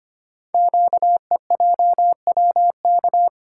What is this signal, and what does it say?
Morse code, heard as a keyed tone, about 25 wpm, 700 Hz. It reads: QEJWK
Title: Code Groups - 5 Letters - 25wpm